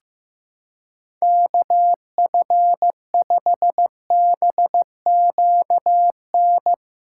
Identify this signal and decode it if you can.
Morse code, heard as a keyed tone, about 15 wpm, 700 Hz. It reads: KF5BQN
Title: USA Callsign DE Callsign - 6 Characters - 15wpm